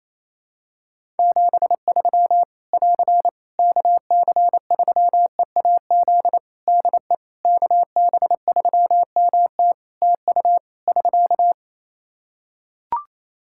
Morse code, heard as a keyed tone, about 28 wpm, 700 Hz. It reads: 73 <AR> KC3EAZ DE KB3MT TU <SK>
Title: Practice QSO - Exchange 4 - Ending - 28wpm